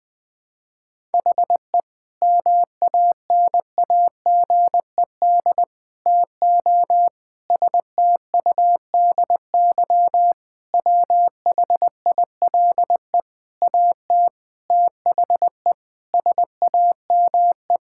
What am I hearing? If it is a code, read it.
Morse code, heard as a keyed tone, about 20 wpm, 700 Hz. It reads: HE MANAGED TO STUDY WHILE AT THE SAME
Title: Sentences with 5-Word Phrases - 20wpm